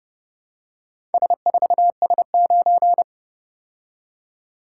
Morse code, heard as a keyed tone, about 29 wpm, 700 Hz. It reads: S4S9